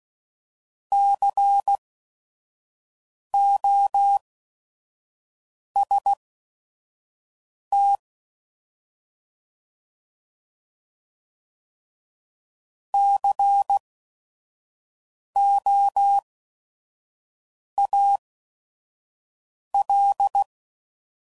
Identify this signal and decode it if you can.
Morse code, heard as a keyed tone, about 16 wpm, spaced as for 5 wpm, 800 Hz. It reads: COST COAL
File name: Prefixes / By First Letter / Cs